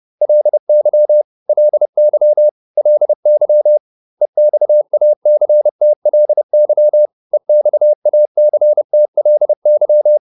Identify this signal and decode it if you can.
Morse code, heard as a keyed tone, about 30 wpm, 600 Hz. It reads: LY LY LY EXACTLY EXACTLY